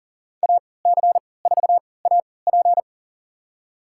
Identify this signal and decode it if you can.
Morse code, heard as a keyed tone, about 40 wpm, spaced as for 23 wpm, 700 Hz. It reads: AC4AP
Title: USA Call Signs - 5 Characters - 3x Character Spacing - 40wpm